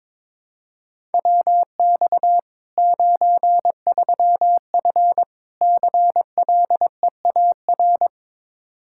Morse code, heard as a keyed tone, about 22 wpm, 700 Hz. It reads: WX 93F CLEAR
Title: Practice QSO - Exchange 2 - Info - 22wpm